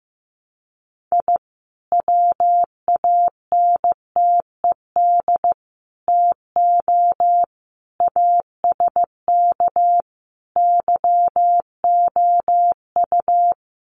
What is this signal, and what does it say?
Morse code, heard as a keyed tone, about 15 wpm, 700 Hz. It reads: I WANTED TO ASK YOU
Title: Sets of 5 Words Encore2 15wpm